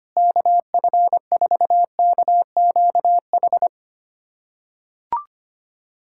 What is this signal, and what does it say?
Morse code, heard as a keyed tone, about 25 wpm, 700 Hz. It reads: KF4KQH
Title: USA Callsign DE Callsign - 6 Characters - 25wpm